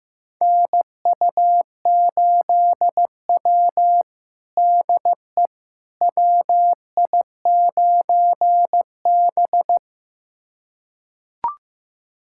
Morse code, heard as a keyed tone, about 15 wpm, 700 Hz. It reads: NU8W DE WI9B
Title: USA Callsign DE Callsign - 4 Characters - 15wpm